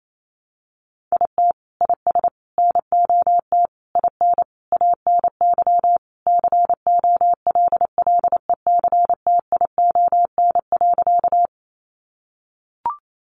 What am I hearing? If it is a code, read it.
Morse code, heard as a keyed tone, about 28 wpm, 700 Hz. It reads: IT IS NOT IN ANY COLLECTION.
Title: Sentences from Top 1200 Words 28wpm